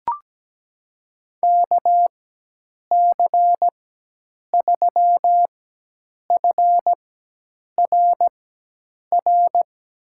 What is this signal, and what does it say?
Morse code, heard as a keyed tone, about 17 wpm, spaced as for 8 wpm, 700 Hz. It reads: KC3FRR
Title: USA Call Signs - 6 Characters - 4x Character Spacing - 17wpm